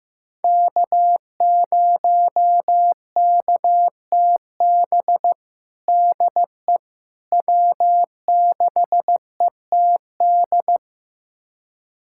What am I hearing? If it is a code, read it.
Morse code, heard as a keyed tone, about 15 wpm, 700 Hz. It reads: K0KTB DE W6ETD
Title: USA Callsign DE Callsign - 5 Characters - 15wpm